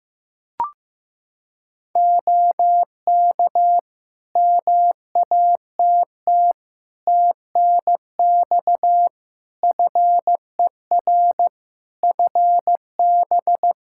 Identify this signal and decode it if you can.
Morse code, heard as a keyed tone, about 15 wpm, 700 Hz. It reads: OK MATT TNX FER FB
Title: Practice QSO - Exchange 4 - Ending - 15wpm